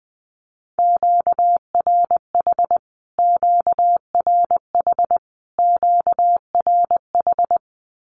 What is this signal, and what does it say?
Morse code, heard as a keyed tone, about 20 wpm, 700 Hz. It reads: QRH QRH QRH
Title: Q-Codes Repeated 3x 20wpm